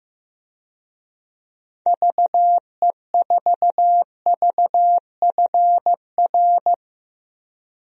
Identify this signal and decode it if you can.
Morse code, heard as a keyed tone, about 16 wpm, 700 Hz. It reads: VE4VFR
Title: DXCC-100 Call Signs - 6 Characters - Repeated 2x - 15wpm